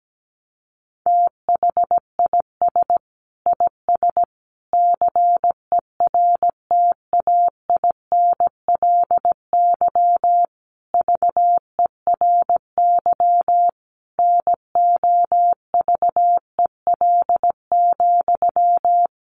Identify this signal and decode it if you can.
Morse code, heard as a keyed tone, about 17 wpm, 700 Hz. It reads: THIS IS CERTAINLY VERY NOVEL,
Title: Sentences from Top 1300 Words 17wpm